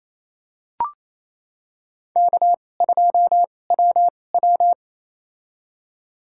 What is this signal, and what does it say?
Morse code, heard as a keyed tone, about 28 wpm, spaced as for 20 wpm, 700 Hz. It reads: K2WW
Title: USA Call Signs - 2x Character Spacing - 28wpm